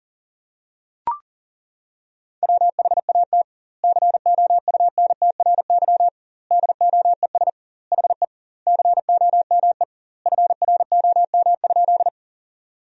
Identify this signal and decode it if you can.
Morse code, heard as a keyed tone, about 40 wpm, 700 Hz. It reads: WHAT COUNTRY DOES HE COME FROM?
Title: 2x Word-Spacing - Sentences from Top 300 Words - 40wpm